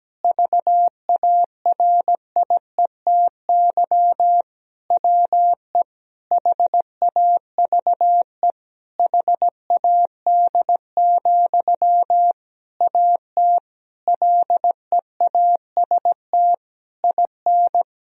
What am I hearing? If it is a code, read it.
Morse code, heard as a keyed tone, about 17 wpm, 700 Hz. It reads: VARIETY WE HAVE HAD, AT LEAST IN
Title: Sentences from Top 1000 Words - Review Part 4 - 17wpm